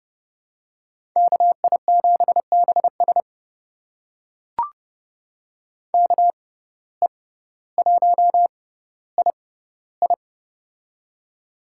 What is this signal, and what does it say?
Morse code, heard as a keyed tone, about 30 wpm, spaced as for 15 wpm, 700 Hz. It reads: KI7BS KE1II
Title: USA Call Signs - 5 Characters - 6x Character Spacing - 30wpm